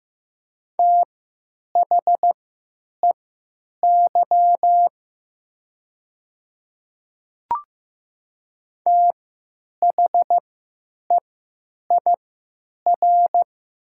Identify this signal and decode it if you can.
Morse code, heard as a keyed tone, about 15 wpm, spaced as for 9 wpm, 700 Hz. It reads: THEY THEIR
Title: 3x Character Spacing - Top 100 Words - 15wpm